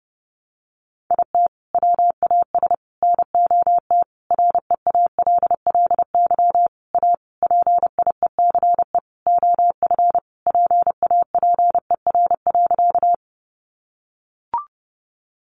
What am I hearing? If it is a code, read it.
Morse code, heard as a keyed tone, about 30 wpm, 700 Hz. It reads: IT WAS NOT REALLY A PIECE OF PAPER.
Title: Sentences from Top 600 Words 30wpm